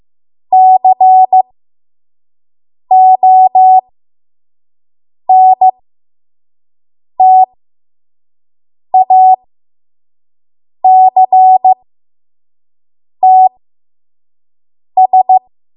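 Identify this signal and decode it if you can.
Morse code, heard as a keyed tone, about 15 wpm, spaced as for 5 wpm, 750 Hz. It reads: CONTACTS
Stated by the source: 5 WPM Morse code practice.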